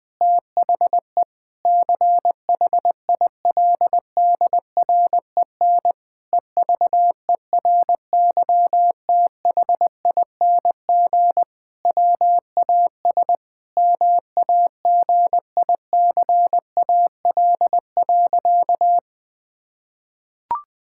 Morse code, heard as a keyed tone, about 20 wpm, 700 Hz. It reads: THE CHILDREN EVERYTHING WAS MAGICAL.
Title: Sentences with 5-Word Phrases - 20wpm